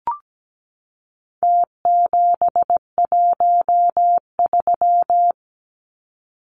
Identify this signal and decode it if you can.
Morse code, heard as a keyed tone, about 17 wpm, 700 Hz. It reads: T713
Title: Numbers - Contest Style - 4-Digit 17wpm